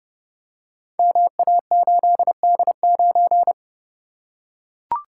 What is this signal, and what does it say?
Morse code, heard as a keyed tone, about 30 wpm, 700 Hz. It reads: MA8D9
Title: Code Groups - 5 Letter:Numbers - 30wpm